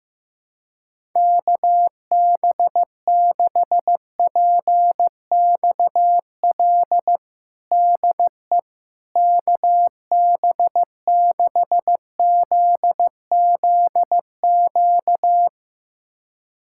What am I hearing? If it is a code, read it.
Morse code, heard as a keyed tone, about 15 wpm, 700 Hz. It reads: KB6PXL DE KB6ZZQ
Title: USA Callsign DE Callsign - 6 Characters - 15wpm